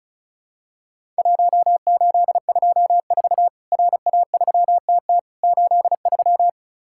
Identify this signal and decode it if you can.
Morse code, heard as a keyed tone, about 36 wpm, 700 Hz. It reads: 1824 RA3TT 83
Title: Worked All Europe DX Contest - QTC Exchange - 35wpm